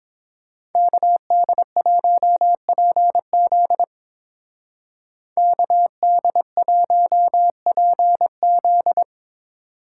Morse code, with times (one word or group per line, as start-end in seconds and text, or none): 0.8-3.8 KD1PZ
5.4-9.0 KD1PZ